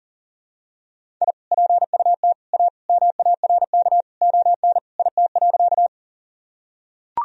Morse code, heard as a keyed tone, about 40 wpm, 700 Hz. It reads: I PUT A MARK ON IT.
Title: Sentences from Top 1000 Words - Review Part 1 - 40wpm